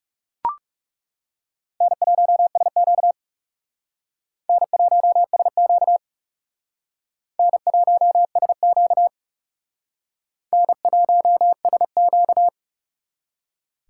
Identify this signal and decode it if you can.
Morse code, heard as a keyed tone, about 37 wpm, 700 Hz. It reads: N1SQ N1SQ N1SQ N1SQ
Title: Speed Racing - USA 1x2 Call Signs 30wpm